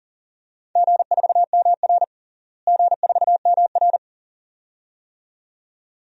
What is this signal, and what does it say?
Morse code, heard as a keyed tone, about 40 wpm, 700 Hz. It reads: G4MR G4MR
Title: DXCC-100 Call Signs - 4 Characters - Repeated 2x - 40wpm